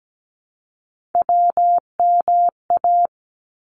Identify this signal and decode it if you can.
Morse code, heard as a keyed tone, about 17 wpm, 700 Hz. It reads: WMA
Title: ARRL Sections 17wpm